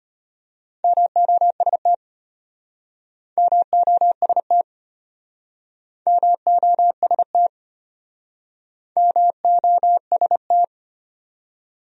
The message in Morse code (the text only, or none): MOST MOST MOST MOST